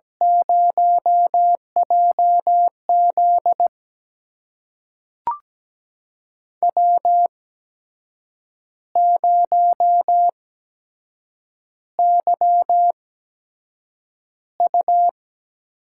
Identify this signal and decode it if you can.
Morse code, heard as a keyed tone, about 17 wpm, spaced as for 7 wpm, 700 Hz. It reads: 0JZ W0YU
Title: USA Call Signs - 4 Characters - 8x Character Spacing - 17wpm